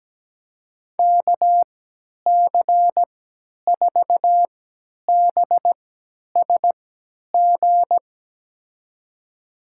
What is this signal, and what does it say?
Morse code, heard as a keyed tone, about 17 wpm, spaced as for 10 wpm, 700 Hz. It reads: KC4BSG
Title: USA Call Signs - 6 Characters - 3x Character Spacing - 17wpm